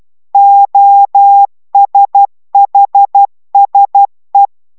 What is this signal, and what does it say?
Morse code, heard as a keyed tone, about 12 wpm, 800 Hz. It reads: OSHSE